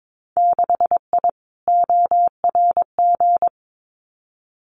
Morse code, heard as a keyed tone, about 22 wpm, 700 Hz. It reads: 6I ORG
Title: Winter Field Day Exchanges Encore 22wpm